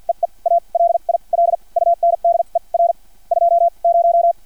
Morse code, heard as a keyed tone, about 50 wpm, spaced as for 38 wpm, 680 Hz. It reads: EE A LIL UNDER 20
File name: archived / CRSnet